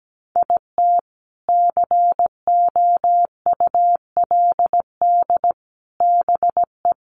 Morse code, heard as a keyed tone, about 17 wpm, 700 Hz. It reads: IT COULD BE
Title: Sets of 3 Words 17wpm